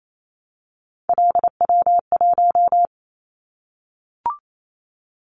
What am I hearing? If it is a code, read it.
Morse code, heard as a keyed tone, about 28 wpm, 700 Hz. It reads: LW1
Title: Callsign Prefixes for Top 50 DXCC - 28wpm